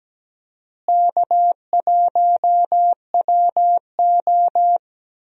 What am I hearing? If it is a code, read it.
Morse code, heard as a keyed tone, about 17 wpm, 700 Hz. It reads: K1WO
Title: USA Call Signs - 4 Characters - Repeated 2x - 17wpm